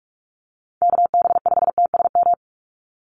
Morse code, heard as a keyed tone, about 46 wpm, 690 Hz. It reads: KB5TSM
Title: USA Call Signs 45wpm